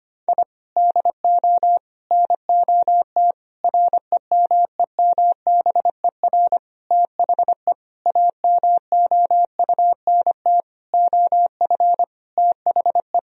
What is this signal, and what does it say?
Morse code, heard as a keyed tone, about 25 wpm, 700 Hz. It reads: I DO NOT REMEMBER THE AMOUNT OF THE
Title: Sentences from Top 1000 Words - Review Part 4 - 25wpm